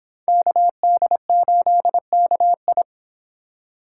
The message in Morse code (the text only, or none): KD8KI